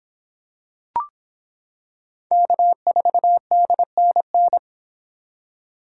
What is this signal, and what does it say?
Morse code, heard as a keyed tone, about 26 wpm, 700 Hz. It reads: K4DNN